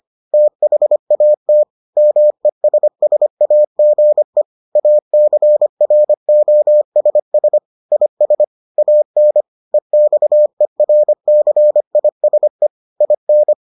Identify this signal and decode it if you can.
Morse code, heard as a keyed tone, about 25 wpm, 600 Hz. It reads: THAT MESSAGE ACROSS IS AN EXERCISE IN